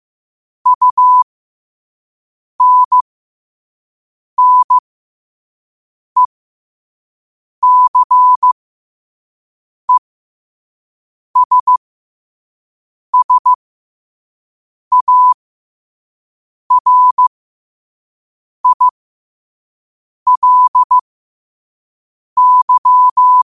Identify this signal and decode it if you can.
Morse code, heard as a keyed tone, about 15 wpm, spaced as for 5 wpm, 1000 Hz. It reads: UNNECESSARILY
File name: qotdfiles / 5wpm